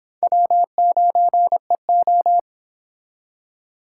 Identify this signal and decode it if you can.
Morse code, heard as a keyed tone, about 26 wpm, 700 Hz. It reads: W9EO